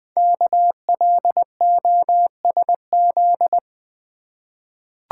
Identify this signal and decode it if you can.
Morse code, heard as a keyed tone, about 20 wpm, 700 Hz. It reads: KLOSZ
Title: Code Groups - 5 Letters - 20wpm